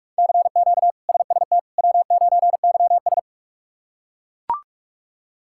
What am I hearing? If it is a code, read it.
Morse code, heard as a keyed tone, about 42 wpm, 700 Hz. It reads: CQ SST W9YS
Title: K1USN SST QSO Practice - 45wpm